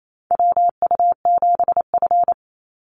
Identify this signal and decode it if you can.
Morse code, heard as a keyed tone, about 28 wpm, 700 Hz. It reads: WU7F
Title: Call Signs - 28wpm